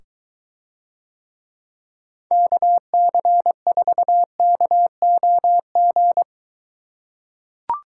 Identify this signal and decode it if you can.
Morse code, heard as a keyed tone, about 23 wpm, 700 Hz. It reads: KC4KOG